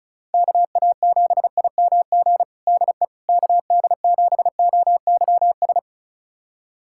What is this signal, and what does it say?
Morse code, heard as a keyed tone, about 34 wpm, 700 Hz. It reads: KA7IMG DE KD7OYS
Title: USA Callsign DE Callsign - 6 Characters - 35wpm